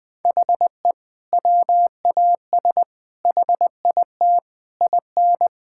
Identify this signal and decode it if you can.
Morse code, heard as a keyed tone, about 20 wpm, 700 Hz. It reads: HE WAS HIT IN
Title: Sentences with 6-Word Phrases - 20wpm